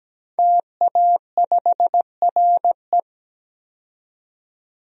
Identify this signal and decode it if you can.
Morse code, heard as a keyed tone, about 17 wpm, 700 Hz. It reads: TA5RE
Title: DXCC-100 Call Signs - 5 Characters - 17wpm